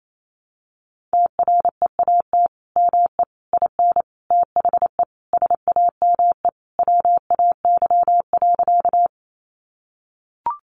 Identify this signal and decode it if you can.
Morse code, heard as a keyed tone, about 28 wpm, 700 Hz. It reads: TREAT ME IN THE SAME WAY.
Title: Sentences from Top 1300 Words 28wpm